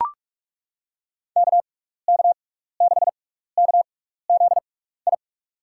Morse code, heard as a keyed tone, about 42 wpm, spaced as for 15 wpm, 700 Hz. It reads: KK6KZI
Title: USA Call Signs - 6 Characters - 6x Character Spacing - 45wpm